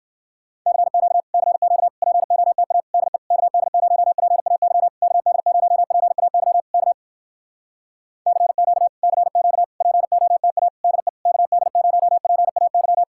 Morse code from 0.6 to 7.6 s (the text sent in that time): CQ CQ POTA DE KD0PAY KD0PAY K